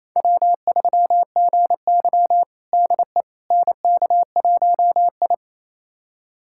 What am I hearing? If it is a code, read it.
Morse code, heard as a keyed tone, about 28 wpm, 700 Hz. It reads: W3GY DE NK1I
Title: USA Callsign DE Callsign - 4 Characters - 28wpm